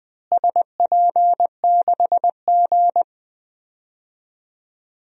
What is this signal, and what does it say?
Morse code, heard as a keyed tone, about 20 wpm, 700 Hz. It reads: SP6G
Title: DXCC-100 Call Signs - 4 Characters - Repeated 2x - 20wpm